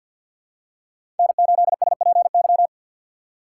Morse code, heard as a keyed tone, about 47 wpm, 700 Hz. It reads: N8SPY
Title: USA Call Signs - 5 Characters - 2x Character Spacing - 50wpm